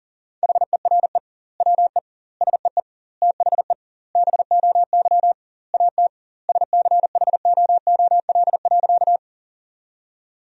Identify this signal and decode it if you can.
Morse code, heard as a keyed tone, about 37 wpm, 700 Hz. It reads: HERE WE SEE THE BOY AT SCHOOL.
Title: 2x Word-Spacing - Sentences from Top 300 Words - 40wpm